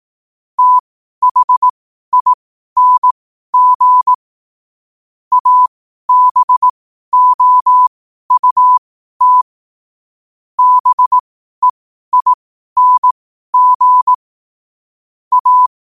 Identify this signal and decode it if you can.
Morse code, heard as a keyed tone, about 18 wpm, spaced as for 12 wpm, 1000 Hz. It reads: THING ABOUT BEING A
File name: qotdfiles / 12wpm